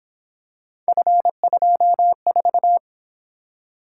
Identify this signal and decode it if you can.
Morse code, heard as a keyed tone, about 26 wpm, 700 Hz. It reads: F24
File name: Speed-Racing - Three Letter-Number Combinations - 22wpm